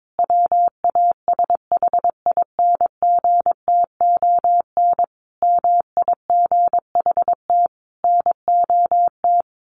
Morse code, heard as a keyed tone, about 22 wpm, 700 Hz. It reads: WASHINGTON MIGHT NOT
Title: A Sound Of Thunder 22wpm